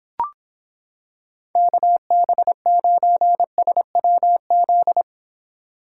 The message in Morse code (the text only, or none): KB9SWZ